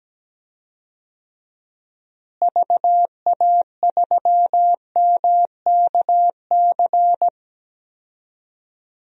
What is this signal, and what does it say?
Morse code, heard as a keyed tone, about 17 wpm, 700 Hz. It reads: VA3MKC
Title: DXCC-100 Call Signs - 6 Characters - Repeated 3x - 17wpm